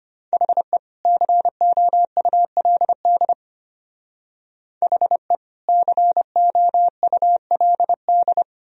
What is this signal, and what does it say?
Morse code, heard as a keyed tone, about 32 wpm, 700 Hz. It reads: HE COULD HE COULD